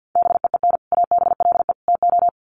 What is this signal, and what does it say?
Morse code, heard as a keyed tone, about 50 wpm, 700 Hz. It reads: BEEN ABLE TO
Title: Sets of 3 Words 50wpm